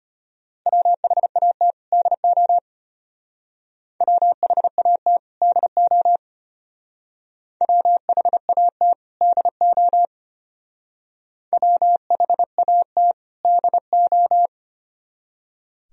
Morse code, heard as a keyed tone, about 29 wpm, 700 Hz. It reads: WHAT DO WHAT DO WHAT DO WHAT DO
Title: Speed-Racing - Sets of 2 Words 25wpm